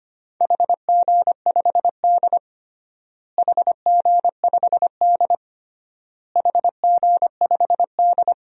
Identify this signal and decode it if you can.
Morse code, heard as a keyed tone, about 25 wpm, 700 Hz. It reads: HG5D HG5D HG5D
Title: DXCC-100 Call Signs - 4 Characters - Repeated 3x - 25wpm